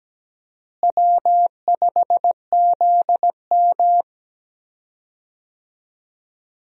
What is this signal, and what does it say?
Morse code, heard as a keyed tone, about 17 wpm, 700 Hz. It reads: W5ZM
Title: USA Call Signs - 4 Characters - Repeated 2x - 17wpm